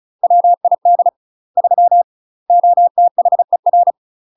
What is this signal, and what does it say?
Morse code, heard as a keyed tone, about 33 wpm, 700 Hz. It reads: WID 3 OTHER
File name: UK MASTER QSOs TRAINING / 35WPM / UK MASTER QSOs 0007